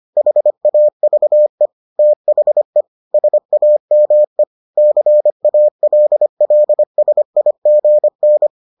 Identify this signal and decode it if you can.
Morse code, heard as a keyed tone, about 25 wpm, 600 Hz. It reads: HAVE THE SAME CALLSIGN